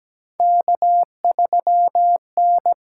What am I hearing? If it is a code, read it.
Morse code, heard as a keyed tone, about 17 wpm, 700 Hz. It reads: K3N
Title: USA Call Signs - 4 Characters - 8x Character Spacing - 17wpm